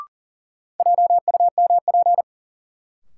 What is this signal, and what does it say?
Morse code, heard as a keyed tone, about 37 wpm, 700 Hz. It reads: JUMP
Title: Sentences from Top 1300 Words - New Word Drill Alt - 40wpm